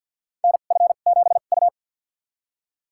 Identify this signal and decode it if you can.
Morse code, heard as a keyed tone, about 49 wpm, spaced as for 36 wpm, 700 Hz. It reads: NF7U
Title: USA Call Signs - 4 Characters - 2x Character Spacing - 50wpm